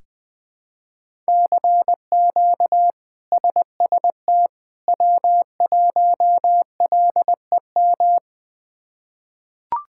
Morse code, heard as a keyed tone, about 20 wpm, 700 Hz. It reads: CQ SST W1LEM
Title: K1USN SST QSO Practice - 20wpm